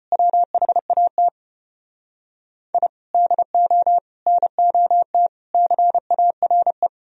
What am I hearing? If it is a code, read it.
Morse code, heard as a keyed tone, about 33 wpm, 700 Hz. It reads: WHAT I DO NOT CARE